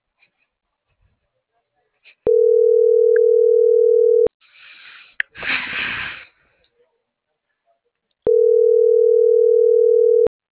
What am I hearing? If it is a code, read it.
Ringback tone (North America), 440 + 480 Hz.
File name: CENTRAL SAGOT / out xquenda.sagot